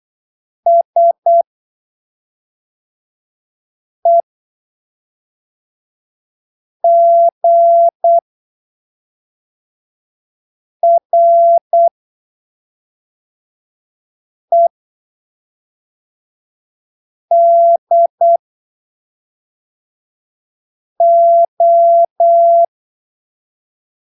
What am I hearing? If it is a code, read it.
Morse code, heard as a keyed tone, about 8 wpm, spaced as for 3 wpm, 680 Hz. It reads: SEGREDO
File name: arquivos / texto / texto 20 08ppm